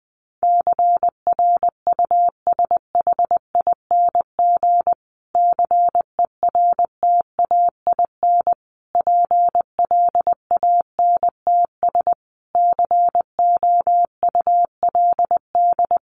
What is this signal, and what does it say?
Morse code, heard as a keyed tone, about 20 wpm, 700 Hz. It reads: CRUSHING CERTAIN PLANTS COULD
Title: A Sound Of Thunder 20wpm